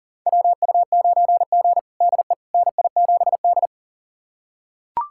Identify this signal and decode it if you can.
Morse code, heard as a keyed tone, about 40 wpm, 700 Hz. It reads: WU9G DE NI7D
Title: USA Callsign DE Callsign - 4 Characters - 40wpm